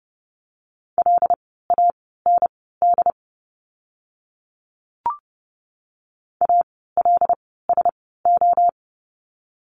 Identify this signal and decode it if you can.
Morse code, heard as a keyed tone, about 30 wpm, spaced as for 17 wpm, 700 Hz. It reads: LAND ALSO
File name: Lesson 17f - Letter u - Words 30wpm